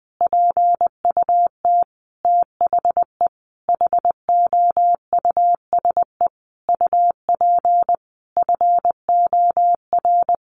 Morse code, heard as a keyed tone, about 20 wpm, 700 Hz. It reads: PUT THE HOUSE UP FOR
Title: Sentences from Top 1400 Words 20wpm